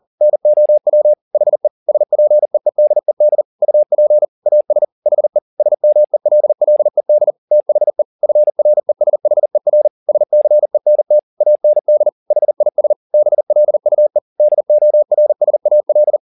Morse code, heard as a keyed tone, about 40 wpm, 600 Hz. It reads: NOW HE SPEEDED UP AS HE SMELLED THE FRESHER SCENT AND HIS BLUE DORSAL
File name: the old man and the sea 2 40wpm